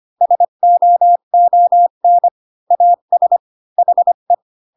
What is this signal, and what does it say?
Morse code, heard as a keyed tone, about 25 wpm, 700 Hz. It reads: SOON AS HE
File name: Pinocchio / Pinocchio ch1 20 25 ik1hge